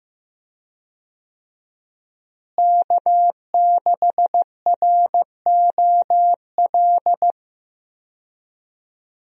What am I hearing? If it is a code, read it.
Morse code, heard as a keyed tone, about 15 wpm, 700 Hz. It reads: K6ROL
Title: USA Call Signs - 5 Characters - Repeated 2x - 15wpm